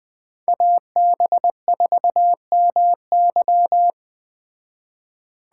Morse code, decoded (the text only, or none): AB4MY